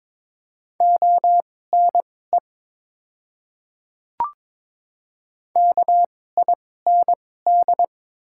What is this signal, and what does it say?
Morse code, heard as a keyed tone, about 22 wpm, spaced as for 16 wpm, 700 Hz. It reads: ONE KIND
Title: 2x Character Spacing - Top 200 Words - 22wpm